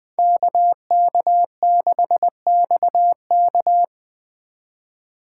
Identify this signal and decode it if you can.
Morse code, heard as a keyed tone, about 20 wpm, 700 Hz. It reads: KK6XK
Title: Speed Racing - USA 2x2 Call Signs 17wpm